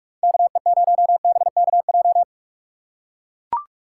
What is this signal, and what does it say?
Morse code, heard as a keyed tone, about 44 wpm, 700 Hz. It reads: KE0BKJ